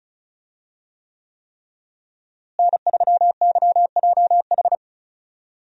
Morse code, heard as a keyed tone, about 36 wpm, 700 Hz. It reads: N3YJH
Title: USA Call Signs - 5 Characters - Repeated 3x - 35wpm